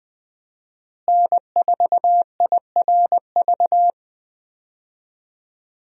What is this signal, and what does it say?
Morse code, heard as a keyed tone, about 21 wpm, 700 Hz. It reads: N4IRV
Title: USA Call Signs - 5 Characters - Repeated 2x - 20wpm